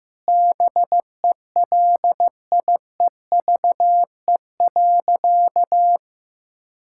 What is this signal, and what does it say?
Morse code, heard as a keyed tone, about 15 wpm, 700 Hz. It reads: BELIEVE.
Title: Sentences from Top 1000 Words - Review Part 6 - 15wpm